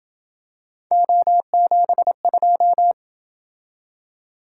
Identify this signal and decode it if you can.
Morse code, heard as a keyed tone, about 28 wpm, 700 Hz. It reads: O72